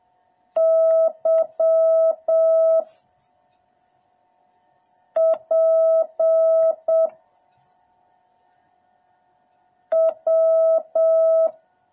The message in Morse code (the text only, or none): Y P W